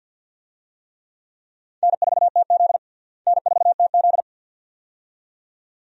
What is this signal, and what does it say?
Morse code, heard as a keyed tone, about 50 wpm, 700 Hz. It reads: N4TZ N4TZ
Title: USA Call Signs - 4 Characters - Repeated 2x - 50wpm